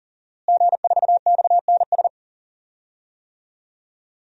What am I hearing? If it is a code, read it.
Morse code, heard as a keyed tone, about 40 wpm, 700 Hz. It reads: G4XNS
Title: DXCC-100 Call Signs - 5 Characters - Repeated 2x - 40wpm